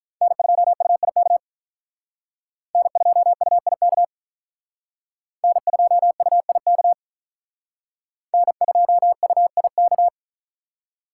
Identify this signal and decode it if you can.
Morse code, heard as a keyed tone, about 44 wpm, 700 Hz. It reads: N2UIK N2UIK N2UIK N2UIK